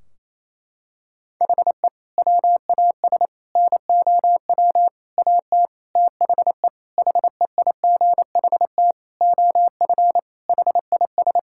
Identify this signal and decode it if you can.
Morse code, heard as a keyed tone, about 28 wpm, 700 Hz. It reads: HE WAS NOW AT THE HEIGHT OF HIS
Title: Sentences from Top 1600 Words 28wpm